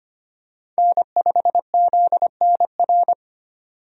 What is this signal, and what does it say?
Morse code, heard as a keyed tone, about 25 wpm, 700 Hz. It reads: N5ZNR
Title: USA Call Signs - 5 Characters - 25wpm